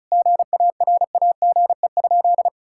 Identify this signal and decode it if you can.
Morse code, heard as a keyed tone, about 34 wpm, 700 Hz. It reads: GARAGE?
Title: Sentences from Top 1600 Words 35wpm